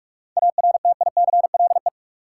Morse code, heard as a keyed tone, about 46 wpm, 700 Hz. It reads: ARTICLE
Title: Sentences from Top 1000 Words - Review Part 5 - New Word Drill - 45wpm